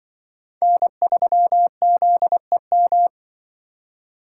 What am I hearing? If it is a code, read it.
Morse code, heard as a keyed tone, about 24 wpm, 700 Hz. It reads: N3ZEM
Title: Speed Racing - USA 1x3 Call Signs 20wpm